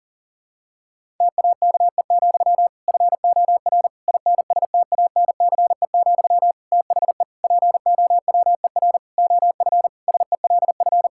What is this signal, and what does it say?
Morse code, heard as a keyed tone, about 40 wpm, 700 Hz. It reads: TAKE, FOR INSTANCE, THE POWER OF SELF
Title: Sentences from Top 1000 Words - Review Part 5 - 40wpm